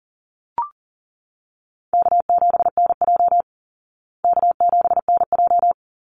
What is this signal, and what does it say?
Morse code, heard as a keyed tone, about 40 wpm, 700 Hz. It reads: K7NJ K7NJ
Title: Call Signs Repeated 2x - 40wpm